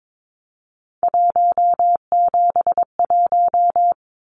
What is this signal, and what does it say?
Morse code, heard as a keyed tone, about 22 wpm, 700 Hz. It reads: 171
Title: Numbers - 3-Digit 22wpm